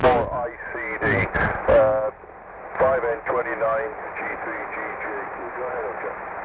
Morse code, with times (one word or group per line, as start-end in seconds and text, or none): none